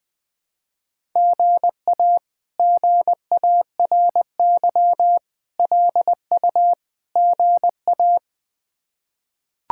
Morse code, heard as a keyed tone, about 21 wpm, 700 Hz. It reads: GA GARY LU GA
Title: K1USN SST QSO Practice - 20wpm